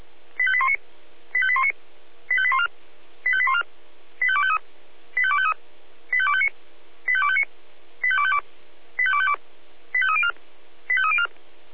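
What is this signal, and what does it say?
Five-tone selective call (ZVEI), heard as quick runs of five tones, tones of about 70 ms. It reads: 85819 85819 85814 85814 85253 85253 85259 85259 85252 85252 85304 85304
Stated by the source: ZVEI five-tone selective call 85218 received by SDR.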